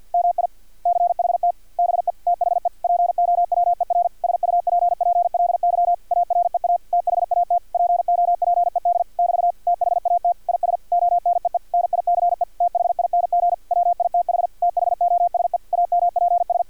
Morse code, heard as a keyed tone, about 50 wpm, 700 Hz. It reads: MICHT BE THE POWER SUPPLY AREA THAT POPED <BT> THAT IS ONEE NICE THING WITH THOSE AMPS
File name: archived / CRSnet